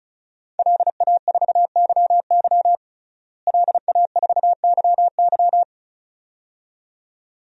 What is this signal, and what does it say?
Morse code, heard as a keyed tone, about 34 wpm, 700 Hz. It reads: LA4YY LA4YY
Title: DXCC-100 Call Signs - 5 Characters - Repeated 2x - 35wpm